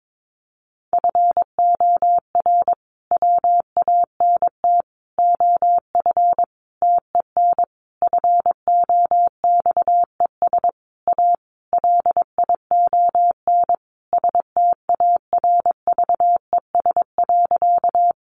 Morse code, heard as a keyed tone, about 22 wpm, 700 Hz. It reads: FOR WANT OF TEN FOXES A LION STARVES.
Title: A Sound Of Thunder 22wpm